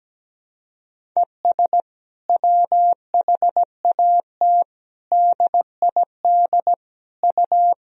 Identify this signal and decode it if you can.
Morse code, heard as a keyed tone, about 17 wpm, 700 Hz. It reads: ES WHAT DID U
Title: Sentences from Top 100 Words and Abbreviations 17wpm